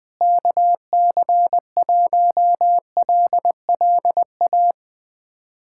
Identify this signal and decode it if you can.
Morse code, heard as a keyed tone, about 20 wpm, 700 Hz. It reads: KC1LLA
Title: USA Callsign DE Callsign 20wpm